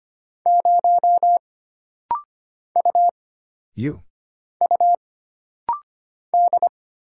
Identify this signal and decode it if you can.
Morse code, heard as a keyed tone, about 25 wpm, 700 Hz. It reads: 0 U U D
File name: Single Character (Letter:Number:Punctuation:Prosign) - Rapid-Fire - 25wpm